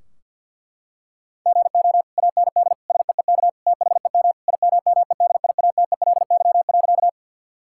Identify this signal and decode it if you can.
Morse code, heard as a keyed tone, about 50 wpm, 700 Hz. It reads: GO AND SEEK THEM IMMEDIATELY.
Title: Sentences from Top 1600 Words 50wpm